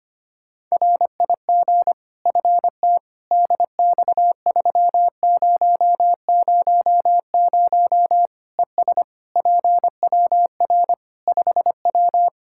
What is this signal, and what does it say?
Morse code, heard as a keyed tone, about 25 wpm, 700 Hz. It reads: RIG FT DX3000 ES PWR 5W
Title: K1USN SST QSO Practice 25wpm